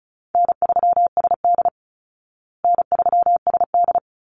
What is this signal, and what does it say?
Morse code, heard as a keyed tone, about 34 wpm, 700 Hz. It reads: N3SD N3SD
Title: Easy Call Signs Repeated 2x - 35wpm